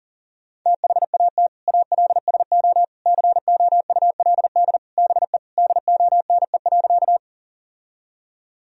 Morse code, heard as a keyed tone, about 40 wpm, 700 Hz. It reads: THAT ALSO COULD BE DONE.
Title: Sentences from Top 1000 Words - Review Part 1 - 40wpm